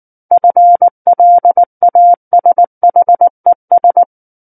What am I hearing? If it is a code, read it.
Morse code, heard as a keyed tone, about 19 wpm, 700 Hz. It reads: FLASHES
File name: Book One / 10-Chapter 10 - 19 WPM